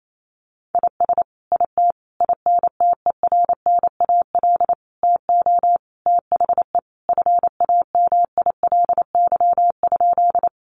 Morse code, heard as a keyed tone, about 28 wpm, 700 Hz. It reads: IS IT INTERNAL TO THE FAMILY?
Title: Sentences from Top 1400 Words 28wpm